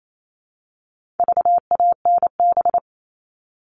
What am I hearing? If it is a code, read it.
Morse code, heard as a keyed tone, about 28 wpm, 700 Hz. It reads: VANB
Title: All Cut Numbers - 4-Digit 28wpm